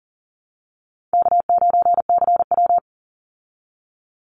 Morse code, heard as a keyed tone, about 39 wpm, 700 Hz. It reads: K9CW
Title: Call Signs Repeated 2x - 40wpm